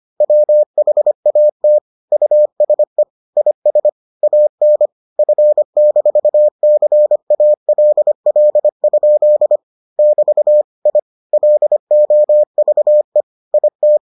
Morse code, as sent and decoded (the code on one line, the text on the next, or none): .-- .... .- - / ..- ... / .. ... / .- -. / ..-. -....- -.-. .- .-.. .-.. ..--.. / -...- / .. / .-.. --- ...- . / .. -
WHAT USE IS AN F-CALL? <BT> I LOVE IT